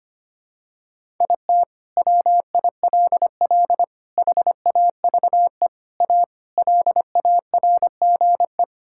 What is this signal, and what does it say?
Morse code, heard as a keyed tone, about 25 wpm, 700 Hz. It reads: IT WILL HAVE A LARGE
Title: Sentences from Top 1000 Words - Review Part 6 - 25wpm